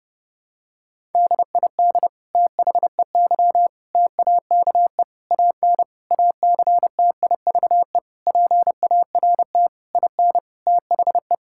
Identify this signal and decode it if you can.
Morse code, heard as a keyed tone, about 30 wpm, 700 Hz. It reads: DID THEY TAKE AN ACTIVE PART IN THE
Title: Sentences from Top 1000 Words - Review Part 6 - 30wpm